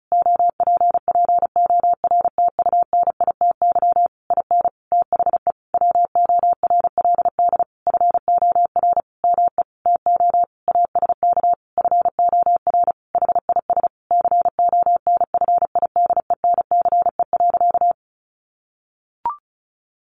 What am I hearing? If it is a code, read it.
Morse code, heard as a keyed tone, about 34 wpm, 700 Hz. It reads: OPPORTUNITY IN THE WORLD FOR ME TO ASK FOR HIS CONFIDENCE.
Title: Sentences from Top 1500 Words 35wpm